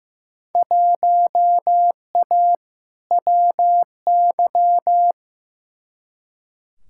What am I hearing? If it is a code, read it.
Morse code, heard as a keyed tone, about 15 wpm, 700 Hz. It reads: 1A WY
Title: Field Day Exchanges Encore - 15wpm